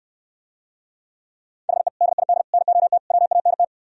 Morse code, heard as a keyed tone, about 66 wpm, 700 Hz. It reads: HE DID NOT WANT